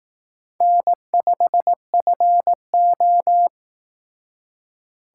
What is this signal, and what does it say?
Morse code, heard as a keyed tone, about 19 wpm, 700 Hz. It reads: N5FO
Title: Speed Racing - USA Call Signs 15wpm